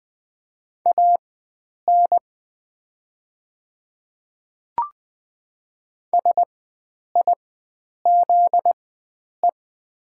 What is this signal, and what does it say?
Morse code, heard as a keyed tone, about 20 wpm, spaced as for 9 wpm, 700 Hz. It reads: AN SIZE
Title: 4x Character Spacing - Top 500 Words - 20wpm